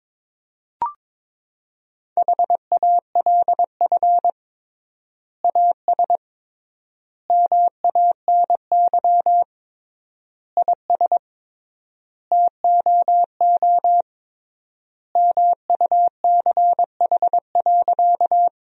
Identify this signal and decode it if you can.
Morse code, heard as a keyed tone, about 22 wpm, 700 Hz. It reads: HALF AS MANY IS TOO MUCH.
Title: 3x Word-Spacing - Sentences from Top 400 Words - 22wpm